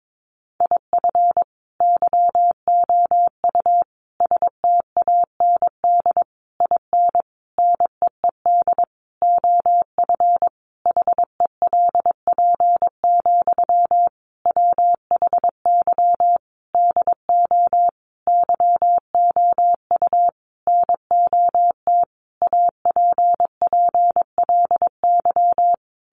Morse code, heard as a keyed tone, about 22 wpm, 700 Hz. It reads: IF YOU STAND IN NEED OF HELP, WHY DO YOU NOT APPLY
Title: Sentences from Top 1200 Words 22wpm